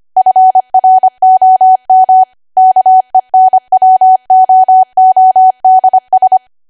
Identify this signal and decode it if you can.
Morse code, heard as a keyed tone, about 25 wpm, 750 Hz. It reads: FROM KENWOODS
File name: Archive / 25 WPM / 25WPM